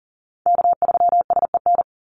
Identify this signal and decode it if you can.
Morse code, heard as a keyed tone, about 40 wpm, 700 Hz. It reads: K3SEN
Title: Call Signs Repeated 2x - 40wpm